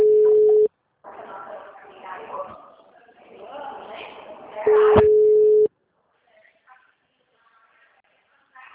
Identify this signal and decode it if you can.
Ringback tone (Europe, 425 Hz), with speech.